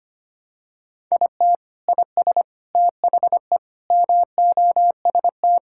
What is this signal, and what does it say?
Morse code, heard as a keyed tone, about 25 wpm, 700 Hz. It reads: IT IS THE MOST
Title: Sentences from Top 1600 Words 25wpm